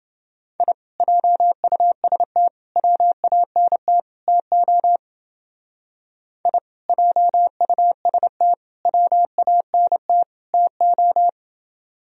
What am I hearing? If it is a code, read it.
Morse code, heard as a keyed tone, about 28 wpm, 700 Hz. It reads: I JUST WANT TO I JUST WANT TO